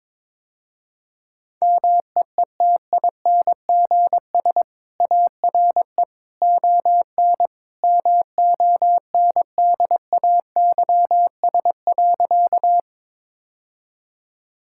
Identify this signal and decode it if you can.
Morse code, heard as a keyed tone, about 22 wpm, 700 Hz. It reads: MEETINGS ARE ON MONDAYS.
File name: practice / 22wpm / Easy Sentences with Hints - 22wpm